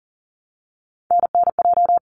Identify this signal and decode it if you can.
Morse code, heard as a keyed tone, about 40 wpm, 700 Hz. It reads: NNJ
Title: ARRL Sections 40wpm